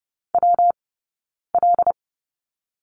Morse code, heard as a keyed tone, about 30 wpm, spaced as for 9 wpm, 690 Hz. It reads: WL
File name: Lesson 19d - Letter w - Two Characters 30wpm